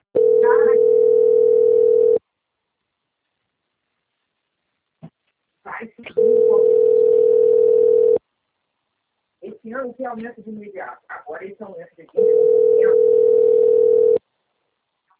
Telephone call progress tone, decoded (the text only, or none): ringback tone